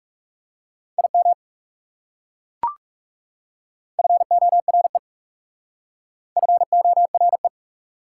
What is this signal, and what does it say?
Morse code, heard as a keyed tone, about 37 wpm, 700 Hz. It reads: IM FORE FORE
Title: Speed-Racing Prefixes and Suffixes 30wpm